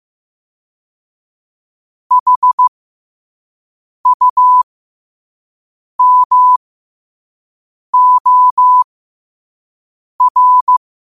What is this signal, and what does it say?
Morse code, heard as a keyed tone, about 15 wpm, spaced as for 5 wpm, 1000 Hz. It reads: HUMOR
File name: qotdfiles / 5wpm